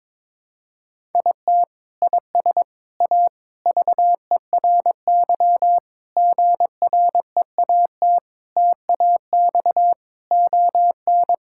Morse code, heard as a keyed tone, about 22 wpm, 700 Hz. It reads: IT IS A VERY GREAT TAX ON
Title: Sentences from Top 1000 Words - Review Part 5 - 22wpm